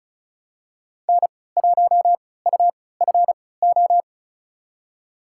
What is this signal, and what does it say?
Morse code, heard as a keyed tone, about 34 wpm, spaced as for 20 wpm, 700 Hz. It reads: N1UFO
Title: USA Call Signs - 5 Characters - 3x Character Spacing - 35wpm